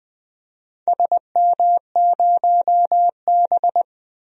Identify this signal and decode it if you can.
Morse code, heard as a keyed tone, about 20 wpm, 700 Hz. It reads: SM0B
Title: DXCC-100 Call Signs - 4 Characters - 20wpm